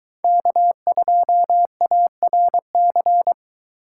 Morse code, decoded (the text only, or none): K2ARC